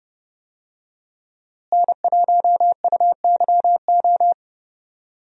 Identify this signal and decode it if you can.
Morse code, heard as a keyed tone, about 30 wpm, 700 Hz. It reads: N1UYO
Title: USA Call Signs - 5 Characters - Repeated 3x - 30wpm